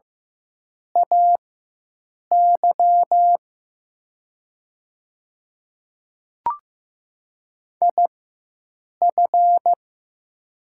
Morse code, heard as a keyed tone, about 15 wpm, spaced as for 7 wpm, 700 Hz. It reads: AY IF
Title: 4x Character Spacing - Top 100 Words - 15wpm